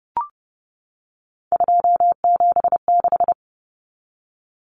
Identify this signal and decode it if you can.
Morse code, heard as a keyed tone, about 30 wpm, 700 Hz. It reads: 276
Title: Numbers - 3-Digit 30wpm